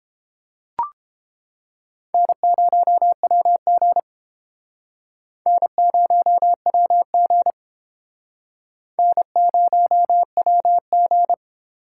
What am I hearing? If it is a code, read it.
Morse code, heard as a keyed tone, about 29 wpm, 700 Hz. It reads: N0WG N0WG N0WG